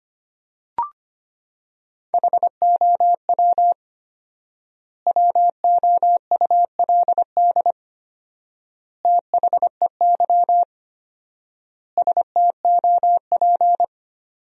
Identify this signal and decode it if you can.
Morse code, heard as a keyed tone, about 25 wpm, 700 Hz. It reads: HOW WOULD THEY STOP
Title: 4x Word-Spacing - Sentences from Top 300 Words - 25wpm